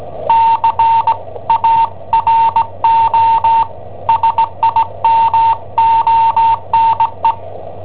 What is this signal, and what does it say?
Morse code, heard as a keyed tone, about 17 wpm, 910 Hz. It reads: CARO SIMONE
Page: Cassetta esercizi preparata da I7OHP per IW5ELP